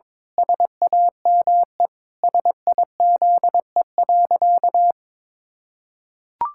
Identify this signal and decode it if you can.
Morse code, heard as a keyed tone, about 22 wpm, 700 Hz. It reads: SAME SIZE.
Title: Pangram Challenge v2 - 22wpm